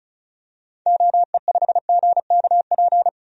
Morse code, heard as a keyed tone, about 36 wpm, 700 Hz. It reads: OE5GKP
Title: DXCC-100 Call Signs - 6 Characters - 35wpm